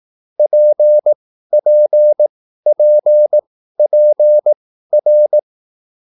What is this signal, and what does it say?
Morse code, heard as a keyed tone, about 18 wpm, spaced as for 13 wpm, 600 Hz. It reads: PPPPR